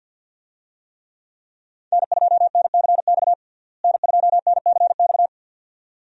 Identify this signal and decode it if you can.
Morse code, heard as a keyed tone, about 50 wpm, 700 Hz. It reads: N2NCX N2NCX
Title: USA Call Signs - 5 Characters - Repeated 2x - 50wpm